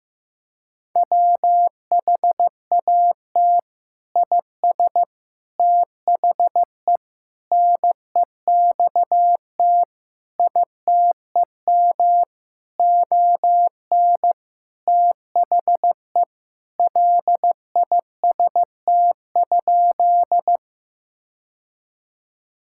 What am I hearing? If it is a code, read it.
Morse code, heard as a keyed tone, about 15 wpm, 700 Hz. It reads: WHAT IS THE NEXT ITEM ON THE LIST?
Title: Sentences from Top 1000 Words - Review Part 6 - 15wpm